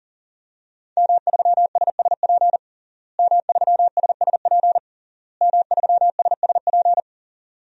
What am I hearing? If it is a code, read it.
Morse code, heard as a keyed tone, about 40 wpm, 700 Hz. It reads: M3SSP M3SSP M3SSP
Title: DXCC-100 Call Signs - 5 Characters - Repeated 3x - 40wpm